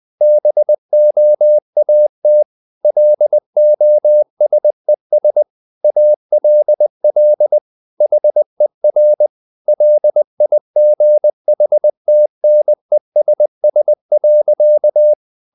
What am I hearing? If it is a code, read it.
Morse code, heard as a keyed tone, about 20 wpm, 600 Hz. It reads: BOAT LOSES ALL HER LIGHTNESS.
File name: the old man and the sea part 2 20wpm